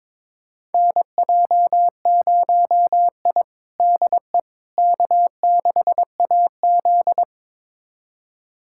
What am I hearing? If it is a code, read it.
Morse code, heard as a keyed tone, about 22 wpm, 700 Hz. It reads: NJ0I DE K6AZ
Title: USA Callsign DE Callsign - 4 Characters - 22wpm